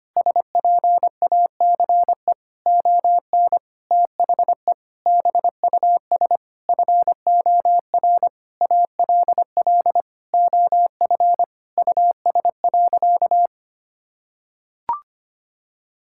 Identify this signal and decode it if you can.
Morse code, heard as a keyed tone, about 25 wpm, 700 Hz. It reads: SPACE ON THE BUS FOR ALL OF US.
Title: Sentences from Top 1000 Words - Review Part 6 - 25wpm